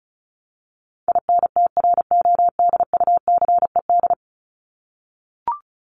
Morse code, heard as a keyed tone, about 34 wpm, 700 Hz. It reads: INTRODUCED
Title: ED Suffix - Top 100 Words with Base Words 35wpm